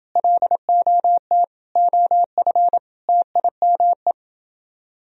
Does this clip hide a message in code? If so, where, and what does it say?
Yes: Morse code from 0.2 to 4.1 s, heard as a keyed tone, about 29 wpm, 700 Hz. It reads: LOT OF TIME